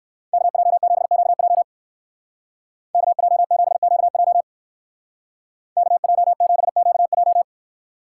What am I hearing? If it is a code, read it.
Morse code, heard as a keyed tone, about 59 wpm, 700 Hz. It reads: KJ7QJ KJ7QJ KJ7QJ